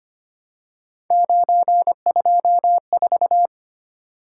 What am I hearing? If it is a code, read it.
Morse code, heard as a keyed tone, about 25 wpm, 700 Hz. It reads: 924
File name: Three Letter-Number Combinations - 25wpm